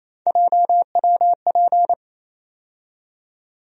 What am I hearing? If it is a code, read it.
Morse code, heard as a keyed tone, about 28 wpm, 700 Hz. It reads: JWP
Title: USA Call Sign Postfixes - 3 Letters 28wpm